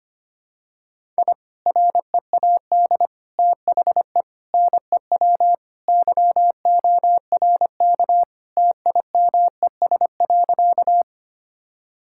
Morse code, heard as a keyed tone, about 25 wpm, 700 Hz. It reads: I READ THE NEW YORK TIMES.
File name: practice / 25wpm / Sentences with 3-Word Phrases - 25wpm